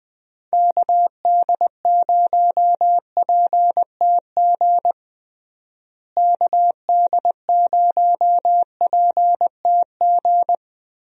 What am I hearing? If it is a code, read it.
Morse code, heard as a keyed tone, about 20 wpm, 700 Hz. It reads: KD0PTG KD0PTG
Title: USA Call Signs - 6 Characters - Repeated 3x - 20wpm